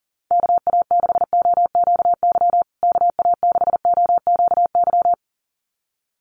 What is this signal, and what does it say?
Morse code, heard as a keyed tone, about 40 wpm, 700 Hz. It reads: KA6OQY KA6OQY
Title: USA Call Signs Repeated 2x 40wpm